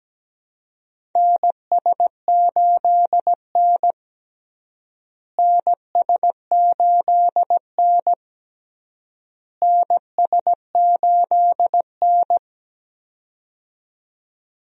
Morse code, heard as a keyed tone, about 17 wpm, 700 Hz. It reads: NS8N NS8N NS8N
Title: USA Call Signs - 4 Characters - Repeated 3x - 17wpm